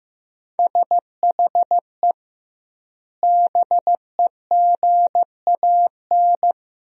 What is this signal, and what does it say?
Morse code, heard as a keyed tone, about 16 wpm, 700 Hz. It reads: SHE BEGAN
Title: 2x Word-Spacing - Sentences from Top 500 Words - 15wpm